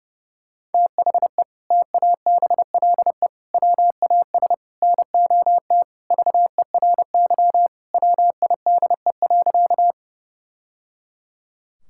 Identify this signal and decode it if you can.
Morse code, heard as a keyed tone, about 30 wpm, 700 Hz. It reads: THE TABLE WAS NOT VERY WIDE.
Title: Sentences from Top 1000 Words - Review Part 6 - 30wpm